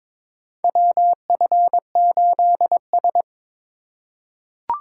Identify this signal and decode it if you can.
Morse code, heard as a keyed tone, about 21 wpm, 700 Hz. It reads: WF8S
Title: USA Call Signs - 4 Characters - 3x Character Spacing - 22wpm